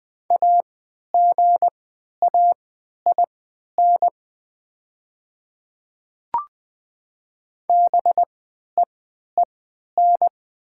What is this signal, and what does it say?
Morse code, heard as a keyed tone, about 20 wpm, spaced as for 11 wpm, 700 Hz. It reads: AGAIN BEEN
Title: 3x Character Spacing - Top 200 Words - 20wpm